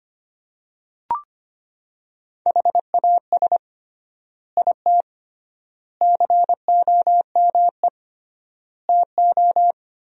Morse code, heard as a keyed tone, about 25 wpm, 700 Hz. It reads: HAS IT COME TO
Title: 3x Word-Spacing - Sentences from Top 100 Words - 25wpm